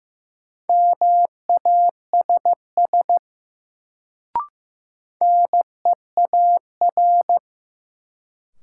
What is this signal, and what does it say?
Morse code, heard as a keyed tone, about 15 wpm, 700 Hz. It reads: MASS NEAR
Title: Sets of 5 Words Encore - New Word Drill - 15wpm